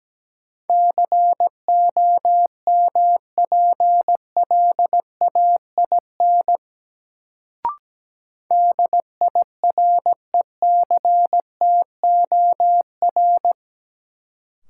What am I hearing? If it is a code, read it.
Morse code, heard as a keyed tone, about 17 wpm, 700 Hz. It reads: COMPLAIN DIRECTOR
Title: Sentences from Top 1300 Words - New Word Drill Alt - 17wpm